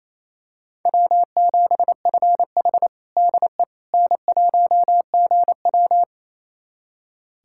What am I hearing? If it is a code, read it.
Morse code, heard as a keyed tone, about 28 wpm, 700 Hz. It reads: W7FH DE N1GW
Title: USA Callsign DE Callsign - 4 Characters - 28wpm